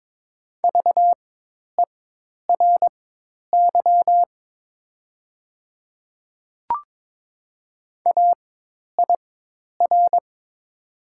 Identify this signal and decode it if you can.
Morse code, heard as a keyed tone, about 22 wpm, spaced as for 10 wpm, 700 Hz. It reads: VERY AIR
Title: 4x Character Spacing - Top 200 Words - 22wpm